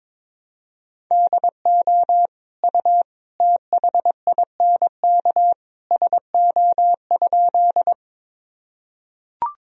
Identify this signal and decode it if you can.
Morse code, heard as a keyed tone, about 22 wpm, 700 Hz. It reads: DO U THINK SO?
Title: Sentences from Top 200 Words and Abbreviations 22wpm